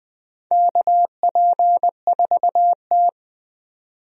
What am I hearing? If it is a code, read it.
Morse code, heard as a keyed tone, about 20 wpm, 700 Hz. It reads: KP4T
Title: DXCC-100 Call Signs - 4 Characters - 20wpm